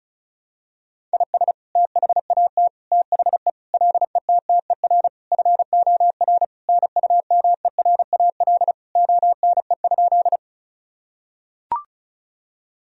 Morse code, heard as a keyed tone, about 34 wpm, 700 Hz. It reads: IS THAT THE LETTER FOR NUMERAL ONE?
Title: Sentences from Top 1000 Words - Review Part 2 - 35wpm